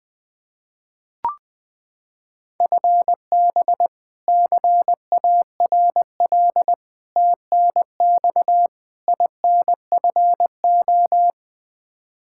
Morse code, heard as a keyed tone, about 21 wpm, 700 Hz. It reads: FB CARL TNX INFO
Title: Practice QSO - Exchange 3 - More Info - 20wpm